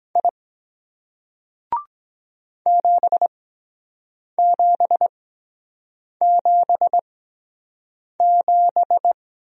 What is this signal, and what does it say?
Morse code, heard as a keyed tone, about 23 wpm, spaced as for 7 wpm, 700 Hz. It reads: I 7777